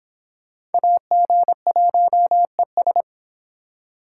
Morse code, heard as a keyed tone, about 32 wpm, 700 Hz. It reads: AG1ES